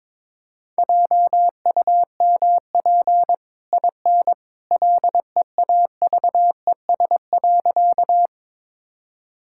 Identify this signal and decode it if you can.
Morse code, heard as a keyed tone, about 22 wpm, 690 Hz. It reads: JUMP IN LEAVES.
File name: practice / 22wpm / Easy Sentences with Hints - 22wpm